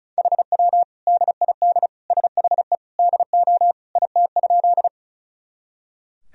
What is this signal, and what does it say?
Morse code, heard as a keyed tone, about 34 wpm, 700 Hz. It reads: HW DID SHE DO IT?
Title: Sentences from Top 100 Words and Abbreviations 35wpm